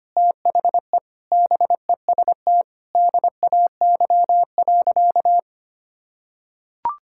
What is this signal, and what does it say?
Morse code, heard as a keyed tone, about 25 wpm, 700 Hz. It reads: THE BEST DAY.
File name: practice / 25wpm / Sentences with 4-Word Phrases - 25wpm